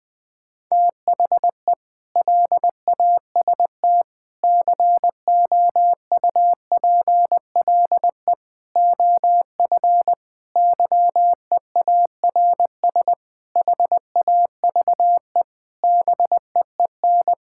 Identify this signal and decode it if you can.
Morse code, heard as a keyed tone, about 20 wpm, 700 Hz. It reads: THE LAST COUPLE OF YEARS HAVE BEEN
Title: Sentences with 5-Word Phrases - Spoken Beginning - 20wpm